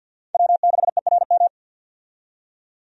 Morse code, heard as a keyed tone, about 50 wpm, 700 Hz. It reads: W6ERM
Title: USA Call Signs - 5 Characters - 50wpm